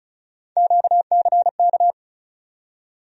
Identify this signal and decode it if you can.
Morse code, heard as a keyed tone, about 36 wpm, 700 Hz. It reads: QCK
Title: USA Call Sign Postfixes - 3 Letters 35wpm